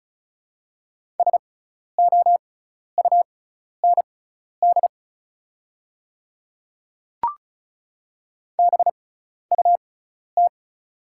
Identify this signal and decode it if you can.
Morse code, heard as a keyed tone, about 36 wpm, spaced as for 12 wpm, 700 Hz. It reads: SOUND BUT
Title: 6x Character Spacing - Top 200 Words - 35wpm